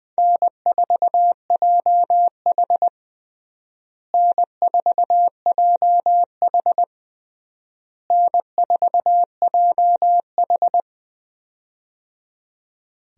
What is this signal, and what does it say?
Morse code, heard as a keyed tone, about 20 wpm, 700 Hz. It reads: N4JH N4JH N4JH
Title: USA Call Signs - 4 Characters - Repeated 3x - 20wpm